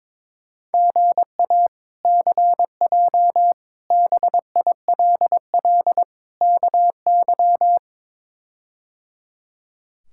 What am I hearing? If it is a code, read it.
Morse code, heard as a keyed tone, about 21 wpm, 700 Hz. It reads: GA CJ BILL KY
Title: K1USN SST QSO Practice - 22wpm